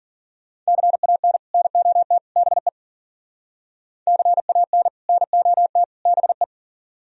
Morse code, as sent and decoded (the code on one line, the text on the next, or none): -.-. .- -. / -. --- - / -... / -.-. .- -. / -. --- - / -... .
CAN NOT BE CAN NOT BE